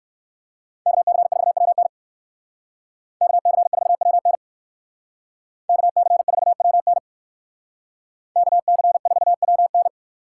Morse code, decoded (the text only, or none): KC4WN KC4WN KC4WN KC4WN